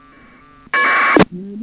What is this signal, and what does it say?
Packet radio (AX.25, as in APRS).
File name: uploader / iss aprs